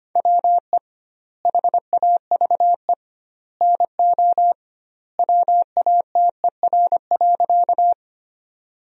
Morse code, heard as a keyed tone, about 25 wpm, 700 Hz. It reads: WE HAVE NO WATER.
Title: 2x Word-Spacing - Sentences from Top 100 Words - 25wpm